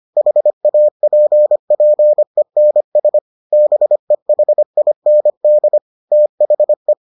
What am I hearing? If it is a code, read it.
Morse code, heard as a keyed tone, about 25 wpm, 600 Hz. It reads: HAPPENS BEHIND THE